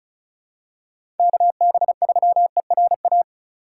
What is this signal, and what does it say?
Morse code, heard as a keyed tone, about 34 wpm, 700 Hz. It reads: KB3ERA
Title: USA Call Signs - 6 Characters - 3x Character Spacing - 35wpm